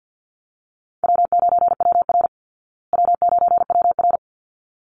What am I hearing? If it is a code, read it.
Morse code, heard as a keyed tone, about 48 wpm, 700 Hz. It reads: W9WR W9WR
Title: Easy Call Signs Repeated 2x - 50wpm